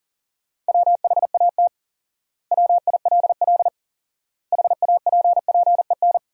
Morse code, heard as a keyed tone, about 40 wpm, 700 Hz. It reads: WHAT WILL HAPPEN
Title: 4x Word-Spacing - Sentences from Top 400 Words - 40wpm